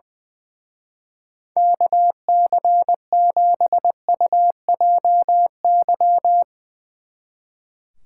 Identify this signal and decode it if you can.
Morse code, heard as a keyed tone, about 22 wpm, 700 Hz. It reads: KC7UJY
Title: Speed Racing - USA 2x3 Call Signs 20wpm